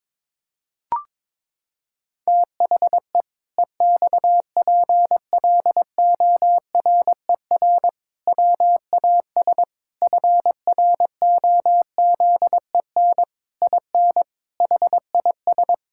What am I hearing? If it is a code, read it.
Morse code, heard as a keyed tone, about 22 wpm, 700 Hz. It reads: THE EXPLORER WAS FROZEN IN HIS
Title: Pangram Challenge v2 - 22wpm